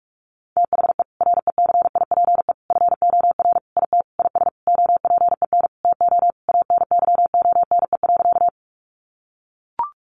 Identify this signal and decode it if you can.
Morse code, heard as a keyed tone, about 44 wpm, 700 Hz. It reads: THE RECIPE FOR IT IS OPEN TO ANYONE.
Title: Sentences from Top 1200 Words 45wpm